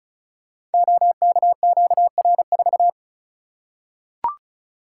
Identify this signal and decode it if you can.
Morse code, heard as a keyed tone, about 36 wpm, 700 Hz. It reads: OKQR4
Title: Code Groups - 5 Letter:Numbers - 35wpm